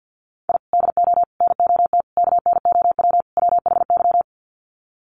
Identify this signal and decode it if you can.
Morse code, heard as a keyed tone, about 49 wpm, 700 Hz. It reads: I DO NOT KNOW WHY
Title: Sets of 5 Words 50wpm